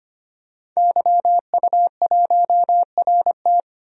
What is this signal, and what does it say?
Morse code, heard as a keyed tone, about 25 wpm, 700 Hz. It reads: YU1RT
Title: DXCC-100 Call Signs - 5 Characters - Repeated 3x - 25wpm